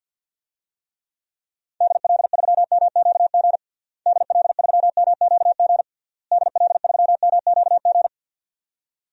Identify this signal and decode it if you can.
Morse code, heard as a keyed tone, about 51 wpm, 700 Hz. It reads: DL3MQG DL3MQG DL3MQG
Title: DXCC-100 Call Signs - 6 Characters - Repeated 3x - 50wpm